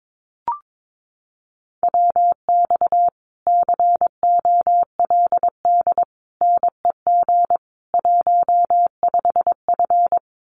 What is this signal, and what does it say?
Morse code, heard as a keyed tone, about 22 wpm, 700 Hz. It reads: WX COLD NEG 15F
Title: Common QSO Phrases Encore 22wpm